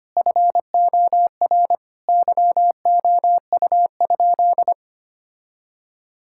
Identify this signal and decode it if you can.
Morse code, heard as a keyed tone, about 25 wpm, 700 Hz. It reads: FOR YOU?
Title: Sentences from Top 1000 Words - Review Part 6 - 25wpm